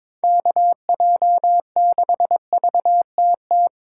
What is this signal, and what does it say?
Morse code, heard as a keyed tone, about 25 wpm, 700 Hz. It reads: KJ6VTT
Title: Speed Racing - USA 2x3 Call Signs 22wpm